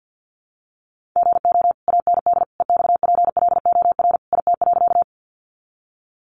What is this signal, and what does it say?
Morse code, heard as a keyed tone, about 50 wpm, 700 Hz. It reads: GO AND EXPLOR IT.
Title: Sentences from Top 1100 Words 50wpm